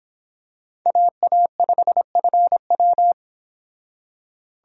Morse code, heard as a keyed tone, about 26 wpm, 700 Hz. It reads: AA5FW